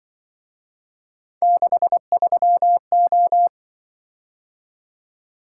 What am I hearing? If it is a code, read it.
Morse code, heard as a keyed tone, about 23 wpm, 700 Hz. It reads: 63O
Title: Speed-Racing - Three Letter-Number Combinations - 20wpm